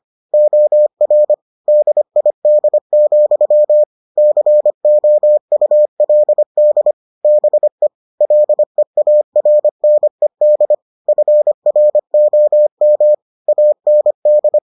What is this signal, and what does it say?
Morse code, heard as a keyed tone, about 25 wpm, 600 Hz. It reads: OR DID, COULD BE LEARNED FROM AND